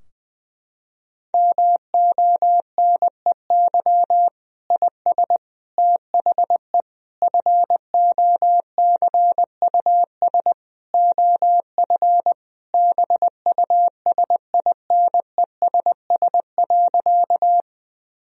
Morse code, heard as a keyed tone, about 20 wpm, 700 Hz. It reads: MONEY IS THE FOCUS OF BUSINESS.
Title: Sentences from Top 1000 Words - Review Part 4 - 20wpm